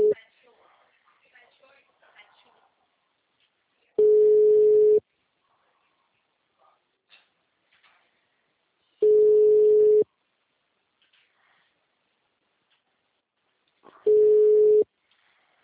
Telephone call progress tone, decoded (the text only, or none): ringback tone